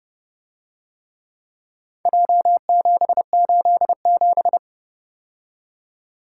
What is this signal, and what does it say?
Morse code, heard as a keyed tone, about 29 wpm, 700 Hz. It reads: J787